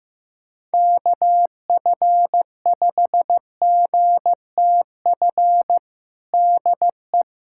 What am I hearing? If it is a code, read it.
Morse code, heard as a keyed tone, about 15 wpm, 700 Hz. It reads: KF5GTF DE
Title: USA Callsign DE Callsign 15wpm